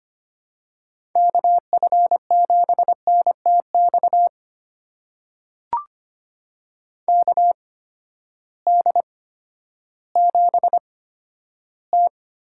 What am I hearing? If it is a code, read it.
Morse code, heard as a keyed tone, about 25 wpm, spaced as for 12 wpm, 700 Hz. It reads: KF7NTX KD7T
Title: USA Call Signs - 6 Characters - 8x Character Spacing - 25wpm